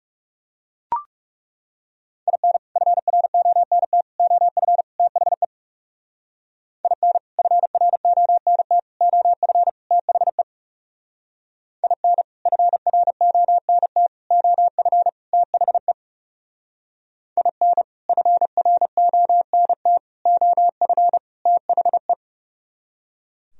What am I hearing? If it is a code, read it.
Morse code, heard as a keyed tone, about 38 wpm, 700 Hz. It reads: IN FRONT OF THE IN FRONT OF THE IN FRONT OF THE IN FRONT OF THE